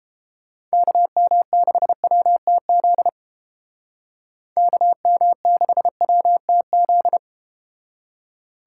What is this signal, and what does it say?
Morse code, heard as a keyed tone, about 30 wpm, 700 Hz. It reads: KM6WTZ KM6WTZ